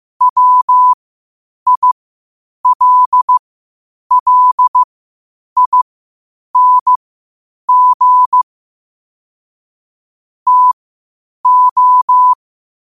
Morse code, heard as a keyed tone, about 15 wpm, spaced as for 8 wpm, 1000 Hz. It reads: WILLING TO
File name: qotdfiles / 8wpm